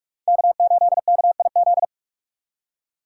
Morse code, heard as a keyed tone, about 44 wpm, 700 Hz. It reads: K8KIZ